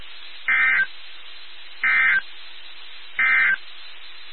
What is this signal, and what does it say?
Emergency Alert System (SAME) tones, heard as harsh buzzing bursts. It is heard from 0.4 to 3.6 s.
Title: NOAA Weather Radio WXK79